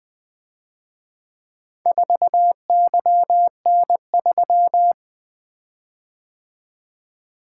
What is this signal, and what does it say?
Morse code, heard as a keyed tone, about 23 wpm, 700 Hz. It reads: 4YN3
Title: Speed-Racing - Four Letter-Number Combinations - 20wpm